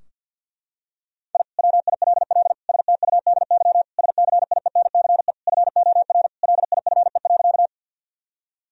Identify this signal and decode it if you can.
Morse code, heard as a keyed tone, about 48 wpm, 700 Hz. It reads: I WILL STUDY SCIENCE FOR LIFE.
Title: Sentences from Top 1000 Words - Review Part 2 - 50wpm